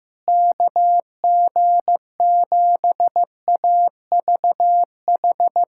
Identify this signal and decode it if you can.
Morse code, heard as a keyed tone, about 15 wpm, 700 Hz. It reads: KG7AVH
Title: USA Callsign DE Callsign 15wpm